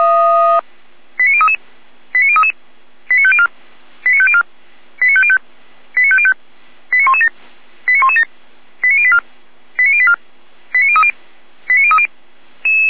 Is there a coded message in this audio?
Yes: five-tone selective call (ZVEI) from 1.2 to 12.1 s, heard as quick runs of five tones, tones of about 70 ms. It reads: 89033 89033 89504 89504 89505 89505 89117 89117 89084 89084 89030 89030